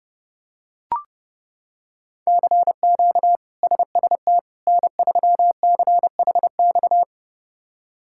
Morse code, heard as a keyed tone, about 30 wpm, 700 Hz. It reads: CQ SST N3CHX
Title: K1USN SST QSO Practice - 30wpm